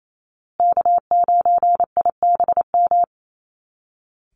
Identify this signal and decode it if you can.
Morse code, heard as a keyed tone, about 28 wpm, 700 Hz. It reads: K9IBM
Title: Call Signs Repeated 2x - 28wpm